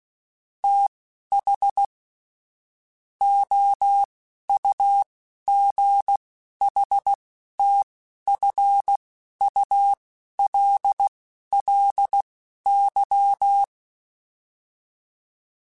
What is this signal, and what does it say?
Morse code, heard as a keyed tone, about 16 wpm, spaced as for 11 wpm, 800 Hz. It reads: TH OUGHTFULLY